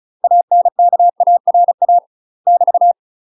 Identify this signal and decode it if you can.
Morse code, heard as a keyed tone, about 34 wpm, 700 Hz. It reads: ANKARA <BT>
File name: UK MASTER QSOs TRAINING/35WPM